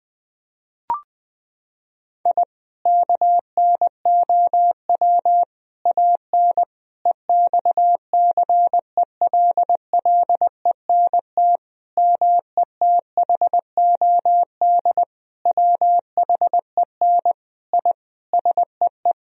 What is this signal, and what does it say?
Morse code, heard as a keyed tone, about 21 wpm, 700 Hz. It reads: I KNOW AN EXCELLENT METHOD WHEN I SEE
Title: Sentences from Top 1000 Words - Review Part 3 - 20wpm